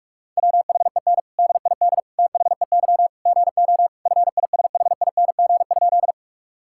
Morse code, heard as a keyed tone, about 41 wpm, 700 Hz. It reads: WHEN DID THEY GO FISHING?
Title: Sentences from Top 1000 Words - Review Part 5 - 45wpm